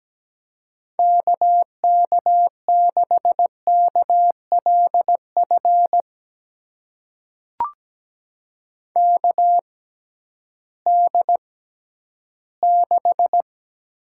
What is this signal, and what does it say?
Morse code, heard as a keyed tone, about 17 wpm, spaced as for 11 wpm, 700 Hz. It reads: KK6KLF KD6
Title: USA Call Signs - 6 Characters - 6x Character Spacing - 17wpm